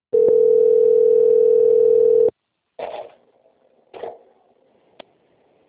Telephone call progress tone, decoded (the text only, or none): ringback tone